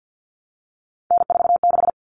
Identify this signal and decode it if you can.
Morse code, heard as a keyed tone, about 50 wpm, 700 Hz. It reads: N46
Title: Common Cut Numbers - 3-Digit 50wpm